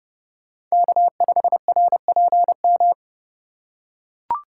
Morse code, heard as a keyed tone, about 30 wpm, 700 Hz. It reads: K5RPM
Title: USA Call Signs - 5 Characters - 4x Character Spacing - 30wpm